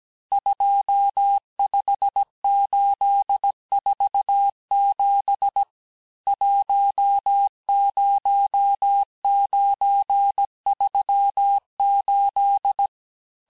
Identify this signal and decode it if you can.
Morse code, heard as a keyed tone, about 17 wpm, 800 Hz. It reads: 25847 10938
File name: cours-du-lundi files / cours-1587-texte-chiffre-vitesse-16MPM